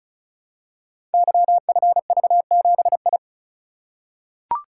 Morse code, heard as a keyed tone, about 36 wpm, 700 Hz. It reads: YFV7I
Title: Code Groups - 5 Letter:Numbers - 35wpm